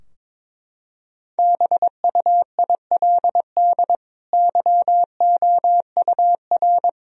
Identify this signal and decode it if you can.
Morse code, heard as a keyed tone, about 22 wpm, 700 Hz. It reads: BUILD YOUR
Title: Sentences from Top 1000 Words - Review Part 2 - 22wpm